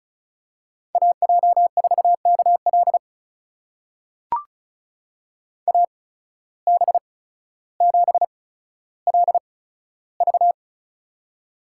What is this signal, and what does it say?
Morse code, heard as a keyed tone, about 34 wpm, spaced as for 15 wpm, 700 Hz. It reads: AJ4KL AB7LV
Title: USA Call Signs - 5 Characters - 8x Character Spacing - 35wpm